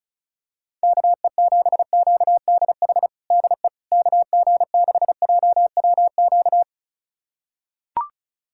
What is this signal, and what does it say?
Morse code, heard as a keyed tone, about 36 wpm, 700 Hz. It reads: KE7QDH DE KG6JWQ
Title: USA Callsign DE Callsign - 6 Characters - 35wpm